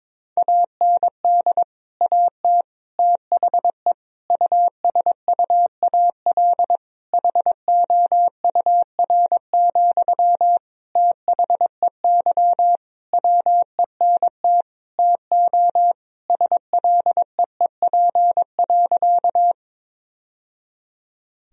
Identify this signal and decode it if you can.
Morse code, heard as a keyed tone, about 22 wpm, 700 Hz. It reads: AND AT THE USUAL HOUR, THEY WENT TO SLEEP.
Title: Sentences from Top 1000 Words - Review Part 2 - 22wpm